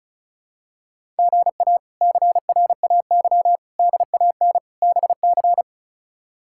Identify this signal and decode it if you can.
Morse code, heard as a keyed tone, about 34 wpm, 700 Hz. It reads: GA CRAY DAN BC
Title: K1USN SST QSO Practice - 35wpm